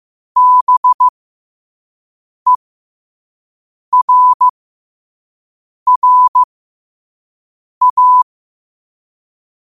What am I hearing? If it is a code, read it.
Morse code, heard as a keyed tone, about 15 wpm, spaced as for 5 wpm, 1000 Hz. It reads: BERRA
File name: 5wpm